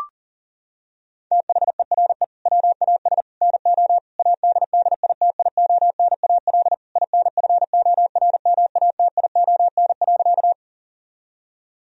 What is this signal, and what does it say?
Morse code, heard as a keyed tone, about 40 wpm, 700 Hz. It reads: THERE WAS NO ADDITIONAL INFORMATION.
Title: Sentences from Top 1000 Words - Review Part 4 - 40wpm